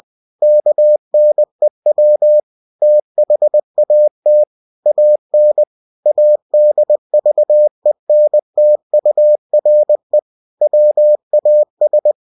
Morse code, heard as a keyed tone, about 20 wpm, 600 Hz. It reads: KNEW THAT AN ADVENTURE WAS
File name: cw-books / winnie-the-pooh / winnie-20wpm-page-0076